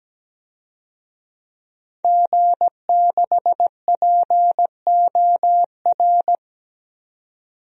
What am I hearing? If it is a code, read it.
Morse code, heard as a keyed tone, about 17 wpm, 700 Hz. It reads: G6POR
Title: DXCC-100 Call Signs - 5 Characters - 17wpm